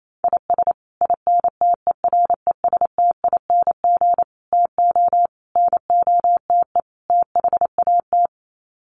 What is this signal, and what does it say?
Morse code, heard as a keyed tone, about 28 wpm, 700 Hz. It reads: IS INTERESTING TO NOTE THAT
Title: Sets of 5 Words Encore2 28wpm